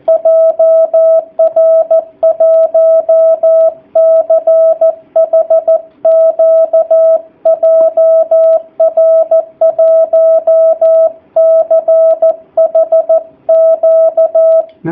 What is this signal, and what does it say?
Morse code, heard as a keyed tone, about 14 wpm, 660 Hz. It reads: JR1CHQ JR1CHQ
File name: jh8cbh / 2019cw / sp13- w